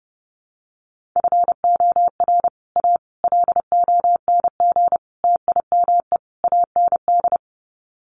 Morse code, heard as a keyed tone, about 30 wpm, 700 Hz. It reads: FOR A LONG TIME AND
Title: Sets of 5 Words Encore2 30wpm